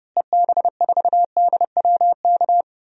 Morse code, heard as a keyed tone, about 32 wpm, 700 Hz. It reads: EB4DWK
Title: DXCC-100 Call Signs - 6 Characters - Repeated 3x - 30wpm